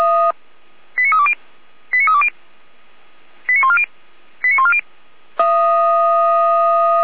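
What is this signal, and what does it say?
Five-tone selective call (ZVEI), heard as quick runs of five tones, tones of about 70 ms. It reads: 89320 89320 89140 89140
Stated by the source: Five-tone ZVEI selective call received by SDR.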